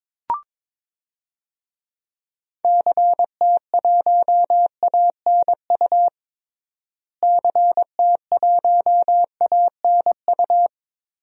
Morse code, heard as a keyed tone, about 21 wpm, 700 Hz. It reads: CT1ANU CT1ANU
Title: DXCC-100 Call Signs - 6 Characters - Repeated 3x - 22wpm